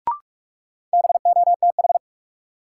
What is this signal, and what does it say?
Morse code, heard as a keyed tone, about 46 wpm, 700 Hz. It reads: BOTH
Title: Sets of 4 Words - New Word Drill - 45wpm